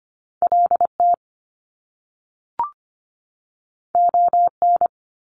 Morse code, heard as a keyed tone, about 25 wpm, 700 Hz. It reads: LT ON
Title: Top Two Letter Combinations in Words 25wpm